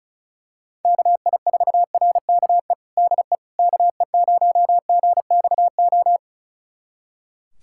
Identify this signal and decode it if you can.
Morse code, heard as a keyed tone, about 35 wpm, 700 Hz. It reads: KI4RKE DE KE0GXO
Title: USA Callsign DE Callsign - 6 Characters - 35wpm